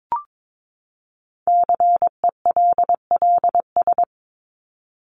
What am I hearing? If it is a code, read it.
Morse code, heard as a keyed tone, about 22 wpm, 700 Hz. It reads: CELLS
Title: ES:S Suffix - Top 100 Words with Base Words 22wpm